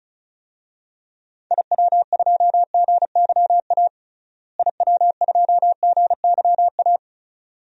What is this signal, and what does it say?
Morse code, heard as a keyed tone, about 36 wpm, 700 Hz. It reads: IW2GYA IW2GYA
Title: DXCC-100 Call Signs - 6 Characters - Repeated 2x - 35wpm